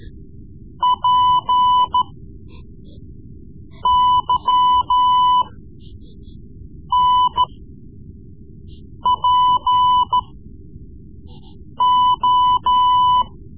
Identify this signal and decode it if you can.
Morse code, heard as a keyed tone, about 14 wpm, spaced as for 5 wpm, 1000 Hz. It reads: PYNPO
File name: Morse Lessons / Morse Lessons / Tape-3B